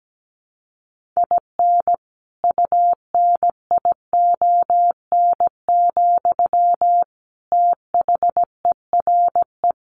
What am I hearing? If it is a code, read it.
Morse code, heard as a keyed tone, about 17 wpm, 700 Hz. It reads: IN UNION, THERE
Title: Sentences from Top 1400 Words 17wpm